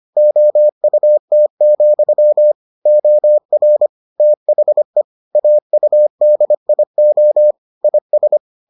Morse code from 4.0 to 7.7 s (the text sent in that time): THE AUDIO